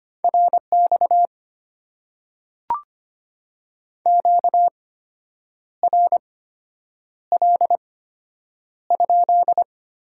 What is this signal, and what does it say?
Morse code, heard as a keyed tone, about 25 wpm, spaced as for 8 wpm, 700 Hz. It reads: RX QRL?
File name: Top 90 QSO Elements - 8x Character Spacing - 25wpm